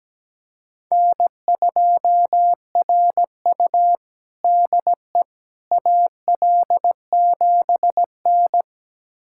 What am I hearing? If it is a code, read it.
Morse code, heard as a keyed tone, about 17 wpm, 700 Hz. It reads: N2RU DE AL7N
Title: USA Callsign DE Callsign - 4 Characters - 17wpm